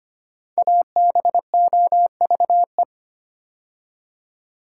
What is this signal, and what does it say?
Morse code, heard as a keyed tone, about 25 wpm, 700 Hz. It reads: ABOVE
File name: 5 Letter Words - First 3 Letters Given - 25wpm